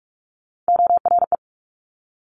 Morse code, heard as a keyed tone, about 46 wpm, 700 Hz. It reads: ORE
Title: Top Three Letter Combinations in Words 45wpm